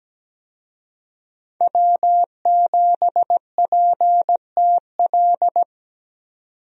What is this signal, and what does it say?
Morse code, heard as a keyed tone, about 17 wpm, 700 Hz. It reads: W7PTL
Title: USA Call Signs - 5 Characters - 17wpm